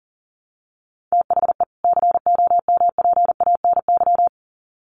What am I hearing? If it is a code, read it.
Morse code, heard as a keyed tone, about 40 wpm, 700 Hz. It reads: THE COMPANY
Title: Sets of 2 Words 40wpm